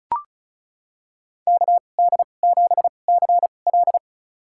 Morse code, heard as a keyed tone, about 36 wpm, spaced as for 25 wpm, 700 Hz. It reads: KD7CL
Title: USA Call Signs - 5 Characters - 2x Character Spacing - 35wpm